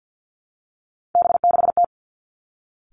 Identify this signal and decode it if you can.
Morse code, heard as a keyed tone, about 50 wpm, 700 Hz. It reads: B6T
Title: Three Challenging Letter-Number Combinations 50wpm